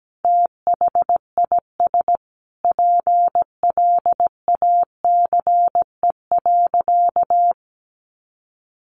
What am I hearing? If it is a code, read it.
Morse code, heard as a keyed tone, about 17 wpm, 700 Hz. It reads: THIS PLACE.
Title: Sentences from Top 1100 Words 17wpm